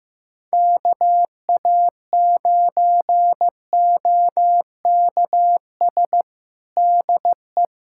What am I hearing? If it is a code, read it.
Morse code, heard as a keyed tone, about 15 wpm, 700 Hz. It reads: KA9OKS DE
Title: USA Callsign DE Callsign - 6 Characters - 15wpm